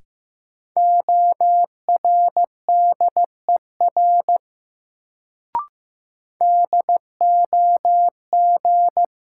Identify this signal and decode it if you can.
Morse code, heard as a keyed tone, about 16 wpm, 700 Hz. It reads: ORDER DOG
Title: Sentences from Top 400 Words - New Word Drill - 15wpm